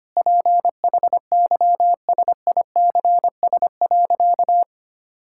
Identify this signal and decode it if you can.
Morse code, heard as a keyed tone, about 25 wpm, 700 Hz. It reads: PHYSICS.
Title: Sentences from Top 1000 Words - Review Part 5 - 25wpm